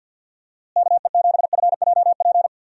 Morse code, heard as a keyed tone, about 52 wpm, 700 Hz. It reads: KE7FJP
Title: USA Call Signs - 6 Characters - Repeated 3x - 50wpm